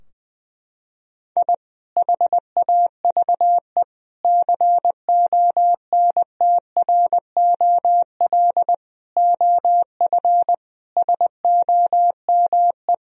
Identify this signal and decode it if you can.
Morse code, heard as a keyed tone, about 20 wpm, 700 Hz. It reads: I HAVE CONTROL OF SOME
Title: Sentences from Top 1000 Words - Review Part 3 - 20wpm